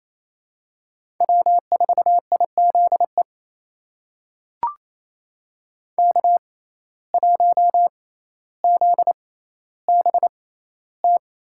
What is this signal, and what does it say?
Morse code, heard as a keyed tone, about 28 wpm, spaced as for 14 wpm, 700 Hz. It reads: W4IZE K1ZBT
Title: USA Call Signs - 5 Characters - 6x Character Spacing - 28wpm